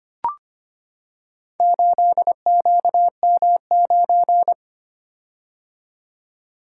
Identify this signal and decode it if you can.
Morse code, heard as a keyed tone, about 25 wpm, 700 Hz. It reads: 8QM9
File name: practice / 25wpm / Four Letter-Number Combinations - 25wpm